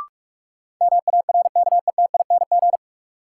Morse code, heard as a keyed tone, about 44 wpm, 700 Hz. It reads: MARKETING
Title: Sentences from Top 1000 Words - Review Part 5 - New Word Drill - 45wpm